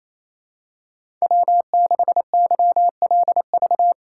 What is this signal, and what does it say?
Morse code, heard as a keyed tone, about 28 wpm, 700 Hz. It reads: W6YLV
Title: USA Call Signs - 5 Characters - 2x Character Spacing - 28wpm